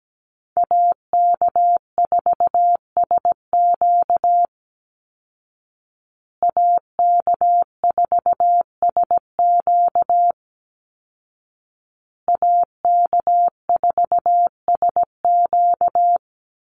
Morse code, heard as a keyed tone, about 17 wpm, 700 Hz. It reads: AK4SQ AK4SQ AK4SQ
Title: Call Signs Repeated 2x - 17wpm